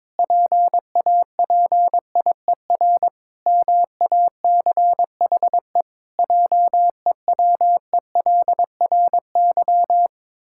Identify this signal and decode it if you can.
Morse code, heard as a keyed tone, about 21 wpm, 700 Hz. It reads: PAPIER MACHE JEWELRY
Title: Pangram Challenge v2 - 22wpm